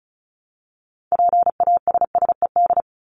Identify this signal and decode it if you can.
Morse code, heard as a keyed tone, about 34 wpm, 700 Hz. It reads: PASSED
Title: ED Suffix - Top 100 Words with Base Words 35wpm